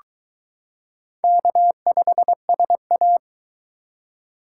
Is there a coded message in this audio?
Yes: Morse code from 1.2 to 3.2 s, heard as a keyed tone, about 22 wpm, 700 Hz. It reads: K5SA